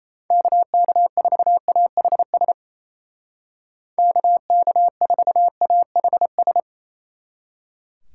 Morse code, heard as a keyed tone, about 33 wpm, 700 Hz. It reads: KK4AHS KK4AHS
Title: Speed Racing - USA Call Signs 28wpm